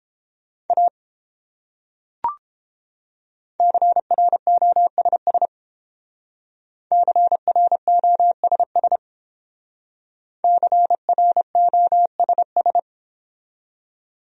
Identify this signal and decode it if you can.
Morse code, heard as a keyed tone, about 32 wpm, 700 Hz. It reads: A CROSS CROSS CROSS